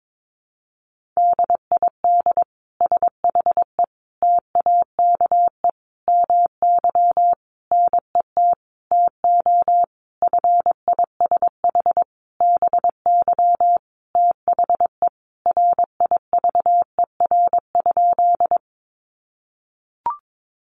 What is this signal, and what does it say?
Morse code, heard as a keyed tone, about 22 wpm, 700 Hz. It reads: DID SHE TAKE MY NET TO FISH BY THE RIVER?
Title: Sentences from Top 1400 Words 22wpm